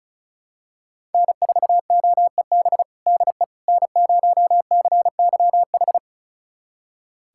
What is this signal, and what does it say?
Morse code, heard as a keyed tone, about 36 wpm, 700 Hz. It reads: N4OEB DE N0CYH
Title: USA Callsign DE Callsign 35wpm